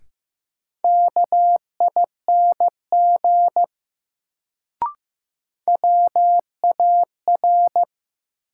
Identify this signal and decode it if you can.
Morse code, heard as a keyed tone, about 15 wpm, 700 Hz. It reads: KING WAR
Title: Sentences from Top 400 Words - New Word Drill - 15wpm